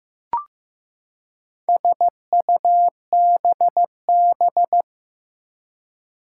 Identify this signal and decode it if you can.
Morse code, heard as a keyed tone, about 15 wpm, 700 Hz. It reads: SUBB
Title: Four Challenging Letter-Number Combinations 15wpm